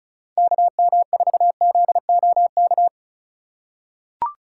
Morse code, heard as a keyed tone, about 36 wpm, 700 Hz. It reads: KM4ZOK
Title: USA Call Signs - 6 Characters - 6x Character Spacing - 35wpm